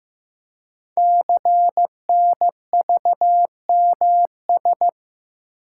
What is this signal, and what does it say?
Morse code, heard as a keyed tone, about 15 wpm, 700 Hz. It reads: CNVMS
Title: Code Groups - 5 Letters - 15wpm